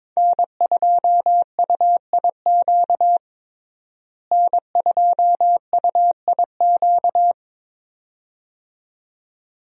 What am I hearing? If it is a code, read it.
Morse code, heard as a keyed tone, about 22 wpm, 700 Hz. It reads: N2UIQ N2UIQ
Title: USA Call Signs - 5 Characters - Repeated 2x - 22wpm